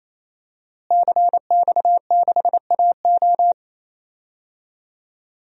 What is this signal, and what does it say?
Morse code, heard as a keyed tone, about 28 wpm, 700 Hz. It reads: CX6AO
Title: DXCC-100 Call Signs - 5 Characters - Repeated 3x - 28wpm